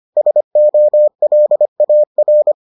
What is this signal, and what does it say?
Morse code, heard as a keyed tone, about 25 wpm, 600 Hz. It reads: SOLAR